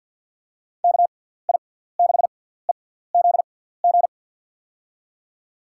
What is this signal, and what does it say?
Morse code, heard as a keyed tone, about 52 wpm, spaced as for 17 wpm, 700 Hz. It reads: KI6EZG
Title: USA Call Signs - 6 Characters - 6x Character Spacing - 50wpm